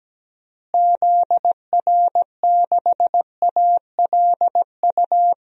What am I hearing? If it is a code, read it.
Morse code, heard as a keyed tone, about 17 wpm, 700 Hz. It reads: ZR6ALU
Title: DXCC-100 Call Signs - 6 Characters - Repeated 3x - 17wpm